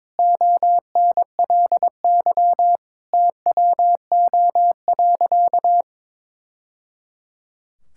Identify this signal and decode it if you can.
Morse code, heard as a keyed tone, about 22 wpm, 700 Hz. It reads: ONLY TWO.
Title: Sentences from Top 1000 Words - Review Part 5 - 22wpm